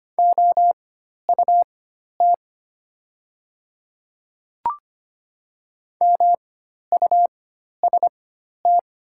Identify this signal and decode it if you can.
Morse code, heard as a keyed tone, about 25 wpm, spaced as for 12 wpm, 700 Hz. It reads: OUT MUST
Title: 4x Character Spacing - Top 200 Words - 25wpm